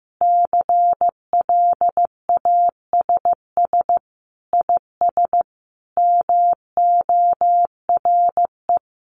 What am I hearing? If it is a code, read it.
Morse code, heard as a keyed tone, about 15 wpm, 700 Hz. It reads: CLASS IS MORE
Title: Sentences from Top 1400 Words 15wpm